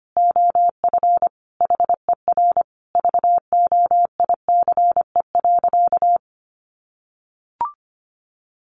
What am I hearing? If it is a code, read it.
Morse code, heard as a keyed tone, about 25 wpm, 700 Hz. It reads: OF HER VOICE.
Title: Sentences from Top 1100 Words 25wpm